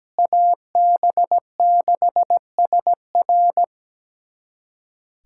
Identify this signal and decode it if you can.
Morse code, heard as a keyed tone, about 17 wpm, 700 Hz. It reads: AB6SR
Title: USA Callsign DE Callsign - 5 Characters - 17wpm